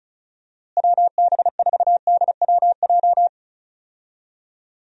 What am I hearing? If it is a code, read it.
Morse code, heard as a keyed tone, about 36 wpm, 700 Hz. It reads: WB4DWJ
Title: USA Call Signs - 6 Characters - Repeated 2x - 35wpm